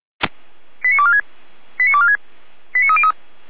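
Five-tone selective call (ZVEI), heard as quick runs of five tones, tones of about 70 ms. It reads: 89246 89246 89303
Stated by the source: ZVEI five-tone call 89130 received by SDR.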